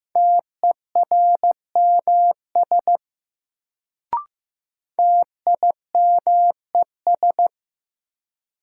Morse code, heard as a keyed tone, about 15 wpm, 700 Hz. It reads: TERMS TIMES
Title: Sets of 3 Words - New Word Drill - 15wpm